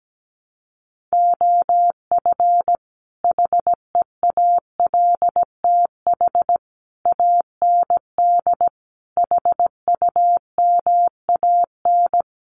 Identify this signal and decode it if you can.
Morse code, heard as a keyed tone, about 17 wpm, 700 Hz. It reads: OF HEALTH AND HUMAN
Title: Sets of 5 Words 17wpm